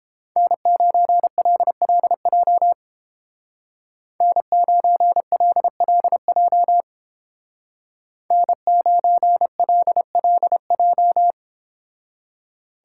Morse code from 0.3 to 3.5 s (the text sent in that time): N9LLJ